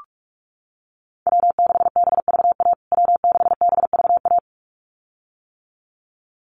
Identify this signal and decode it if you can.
Morse code, heard as a keyed tone, about 44 wpm, 700 Hz. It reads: W6BVA W6BVA
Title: USA Call Signs Repeated 2x 45wpm